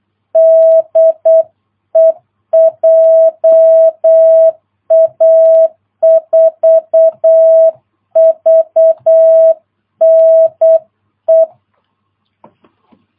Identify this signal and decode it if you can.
Morse code, heard as a keyed tone, about 8 wpm, 660 Hz. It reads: DE JA4VNE
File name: jh8cbh / 2022cw